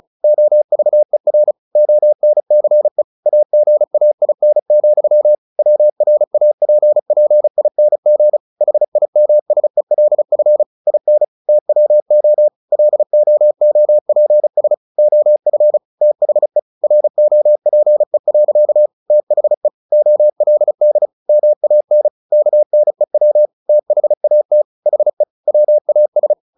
Morse code, heard as a keyed tone, about 35 wpm, 600 Hz. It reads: OVER ONCE AGAIN, WRAPPING HIMSELF IN TWO LOOPS OF THE ROPE. THE OLD MAN KNEW THAT HE WAS
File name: the old man and the sea 2 35wpm